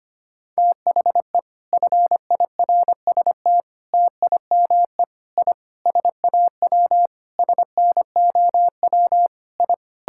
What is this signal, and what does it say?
Morse code, heard as a keyed tone, about 25 wpm, 700 Hz. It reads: THE FIRST TIME I SAW SNOW I
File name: practice / 25wpm / Sentences with 4-Word Phrases - 25wpm